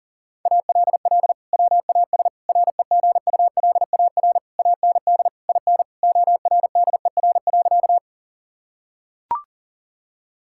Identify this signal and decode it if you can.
Morse code, heard as a keyed tone, about 37 wpm, 700 Hz. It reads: ALL WAS REGULAR AND IN ORDER.
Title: Sentences from Top 1000 Words - Review Part 6 - 40wpm